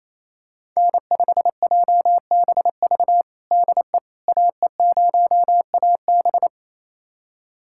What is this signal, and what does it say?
Morse code, heard as a keyed tone, about 28 wpm, 700 Hz. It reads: N5JBV DE AE0AB
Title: USA Callsign DE Callsign - 5 Characters - 28wpm